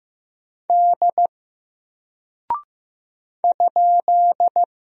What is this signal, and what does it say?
Morse code, heard as a keyed tone, about 15 wpm, 700 Hz. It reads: D ?
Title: Single Character (Letter:Number:Punctuation:Prosign) Alt 15wpm